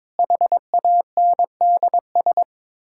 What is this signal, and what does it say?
Morse code, heard as a keyed tone, about 22 wpm, 700 Hz. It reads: HANDS
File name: practice / 22wpm / Sets of 7 Words - 22wpm